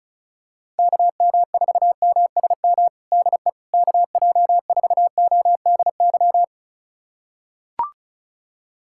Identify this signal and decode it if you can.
Morse code, heard as a keyed tone, about 36 wpm, 700 Hz. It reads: KM4MSM DE KJ4ODY
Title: USA Callsign DE Callsign 35wpm